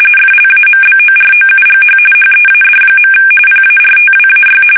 RTTY, about 45.45 baud. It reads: FLAMBOROUGH HEAD. PARTWAY
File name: fsk850 / usnavy1